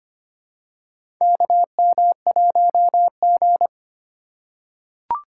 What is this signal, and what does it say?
Morse code, heard as a keyed tone, about 25 wpm, 700 Hz. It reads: KM1G
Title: USA Call Signs - 4 Characters - 3x Character Spacing - 25wpm